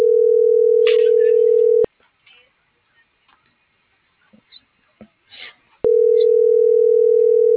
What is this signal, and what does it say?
Ringback tone (North America), 440 + 480 Hz.